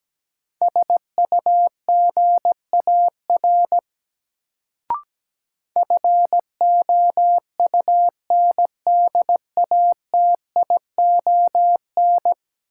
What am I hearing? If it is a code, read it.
Morse code, heard as a keyed tone, about 17 wpm, 700 Hz. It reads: SUGAR FOUNDATION
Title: Sentences from Top 1200 Words - New Word Drill - 17wpm